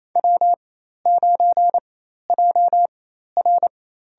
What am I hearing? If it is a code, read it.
Morse code, heard as a keyed tone, about 28 wpm, spaced as for 13 wpm, 700 Hz. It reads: W9JR
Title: USA Call Signs - 4x Character Spacing - 28wpm